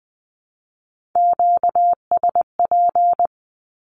Morse code, heard as a keyed tone, about 20 wpm, 700 Hz. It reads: QSP
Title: Q-Codes 20wpm